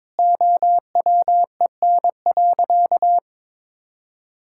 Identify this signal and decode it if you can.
Morse code, heard as a keyed tone, about 22 wpm, 700 Hz. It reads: OWEN.
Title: Pangram Challenge v2 - 22wpm